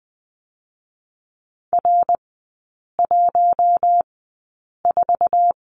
Morse code, heard as a keyed tone, about 20 wpm, spaced as for 8 wpm, 700 Hz. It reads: R14
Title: 067 Lesson 12e - Letter d - Three Characters 20wpm